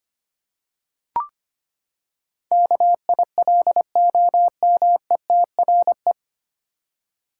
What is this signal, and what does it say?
Morse code, heard as a keyed tone, about 25 wpm, 700 Hz. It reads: KILOMETRE
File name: Nine Letter Words - 25wpm